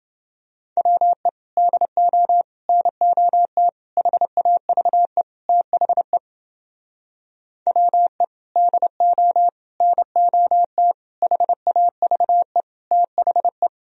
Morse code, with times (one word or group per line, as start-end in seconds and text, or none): 0.8-1.3 WE
1.6-2.4 DO
2.7-3.7 NOT
4.0-5.2 HAVE
5.5-6.2 THE
7.7-8.2 WE
8.6-9.5 DO
9.8-10.9 NOT
11.2-12.6 HAVE
12.9-13.7 THE